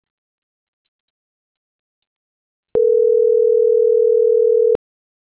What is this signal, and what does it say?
Ringback tone (North America), 440 + 480 Hz.